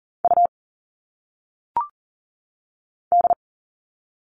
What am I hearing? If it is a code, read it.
Morse code, heard as a keyed tone, about 40 wpm, 700 Hz. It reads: U D
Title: All Cut Numbers - 1-Digit 40wpm